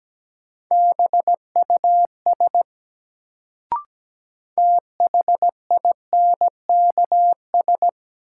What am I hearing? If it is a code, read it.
Morse code, heard as a keyed tone, about 17 wpm, 700 Hz. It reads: BUS THINKS
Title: Sentences from Top 1000 Words - Review Part 6 - New Word Drill - 17wpm